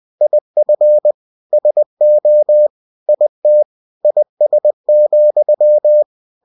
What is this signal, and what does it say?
Morse code, heard as a keyed tone, about 20 wpm, 600 Hz. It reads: IF SO IT IS,
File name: cw-books / winnie-the-pooh / winnie-20wpm-page-0023